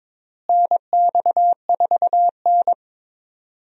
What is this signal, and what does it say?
Morse code, heard as a keyed tone, about 22 wpm, 700 Hz. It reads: NX4N
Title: USA Call Signs - 4 Characters - Repeated 3x - 22wpm